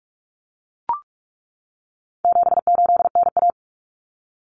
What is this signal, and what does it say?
Morse code, heard as a keyed tone, about 44 wpm, 700 Hz. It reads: 78NA
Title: Common Cut Numbers - 4-Digit 45wpm